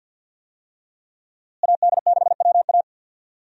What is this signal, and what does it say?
Morse code, heard as a keyed tone, about 51 wpm, 700 Hz. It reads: AD6WA
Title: USA Call Signs - 5 Characters - Repeated 3x - 50wpm